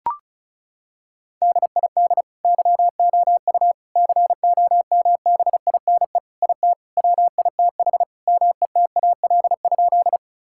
Morse code, heard as a keyed tone, about 34 wpm, 700 Hz. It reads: DID YOU COMBINE IT WITH METAL?
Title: Sentences from Top 1600 Words 35wpm